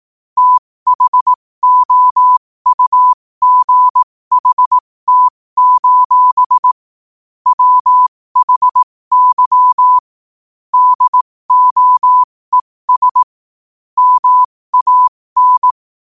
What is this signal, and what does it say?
Morse code, heard as a keyed tone, about 18 wpm, spaced as for 15 wpm, 1000 Hz. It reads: THOUGHT: WHY DOES MAN
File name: qotdfiles / 15wpm